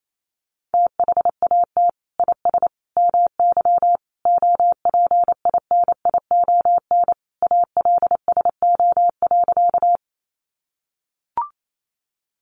Morse code, heard as a keyed tone, about 28 wpm, 700 Hz. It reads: THAT IS MY OPINION ALSO.
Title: Sentences from Top 1200 Words 28wpm